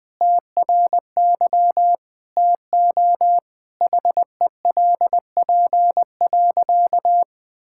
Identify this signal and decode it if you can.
Morse code, heard as a keyed tone, about 20 wpm, 700 Hz. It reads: TRY TO HELP.
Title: Sentences with 3-Word Phrases - Spoken Beginning - 20wpm